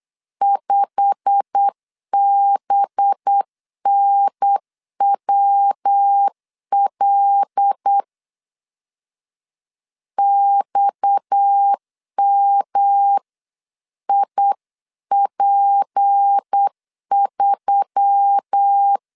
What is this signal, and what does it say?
Morse code, heard as a keyed tone, about 8 wpm, 800 Hz. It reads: 5BNWL XMIP3
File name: CW / DET / 1 9 1500C S C 8wpm 5653